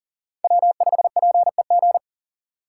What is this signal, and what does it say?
Morse code, heard as a keyed tone, about 40 wpm, 700 Hz. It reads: W5PEG
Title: USA Call Signs - 5 Characters - 6x Character Spacing - 40wpm